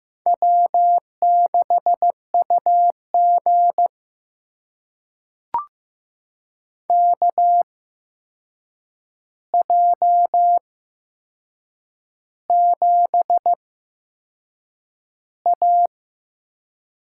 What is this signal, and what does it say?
Morse code, heard as a keyed tone, about 15 wpm, spaced as for 6 wpm, 700 Hz. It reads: W6UG KJ7A
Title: USA Call Signs - 8x Character Spacing - 15wpm